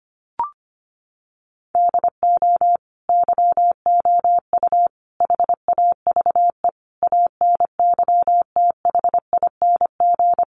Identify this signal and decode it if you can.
Morse code, heard as a keyed tone, about 25 wpm, 700 Hz. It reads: DO YOU HAVE ANYTHING
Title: Sentences from Top 1000 Words 25wpm